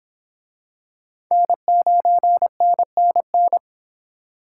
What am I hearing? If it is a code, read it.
Morse code, heard as a keyed tone, about 25 wpm, 700 Hz. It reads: N9NNN